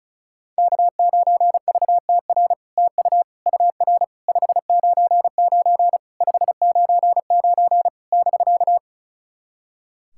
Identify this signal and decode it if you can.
Morse code, heard as a keyed tone, about 36 wpm, 700 Hz. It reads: K9VTR TU UR 599 599 <BK>
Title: SOTA QSO Practice 35wpm